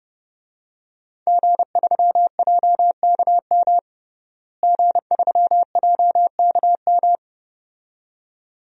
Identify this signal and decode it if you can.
Morse code, heard as a keyed tone, about 30 wpm, 700 Hz. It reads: G3JKM G3JKM
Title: DXCC-100 Call Signs - 5 Characters - Repeated 2x - 30wpm